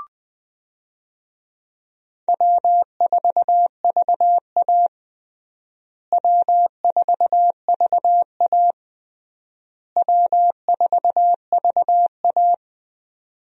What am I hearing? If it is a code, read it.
Morse code, heard as a keyed tone, about 21 wpm, 700 Hz. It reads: W4VA W4VA W4VA
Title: USA Call Signs - 4 Characters - Repeated 3x - 20wpm